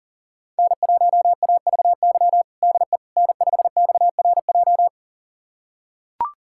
Morse code, heard as a keyed tone, about 40 wpm, 700 Hz. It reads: N1AVY DE N5XRJ
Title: USA Callsign DE Callsign - 5 Characters - 40wpm